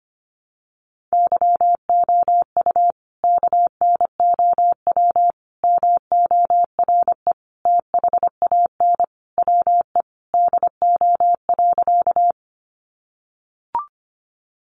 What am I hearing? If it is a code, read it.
Morse code, heard as a keyed tone, about 25 wpm, 700 Hz. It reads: YOU KNOW MORE THAN WE DO.
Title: Sentences from Top 100 Words 25wpm